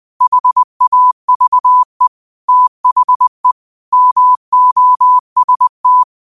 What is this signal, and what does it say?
Morse code, heard as a keyed tone, about 20 wpm, 1000 Hz. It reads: HAVE THE MOST
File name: qotdfiles / 20wpm